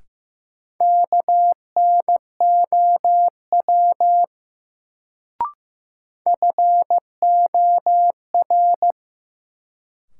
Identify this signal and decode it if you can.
Morse code, heard as a keyed tone, about 15 wpm, 700 Hz. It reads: KNOW FOR
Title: Sentences from Top 100 Words - New Word Drill - 15wpm